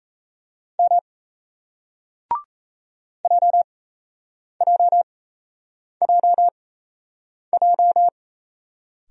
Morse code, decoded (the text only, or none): M JJJJ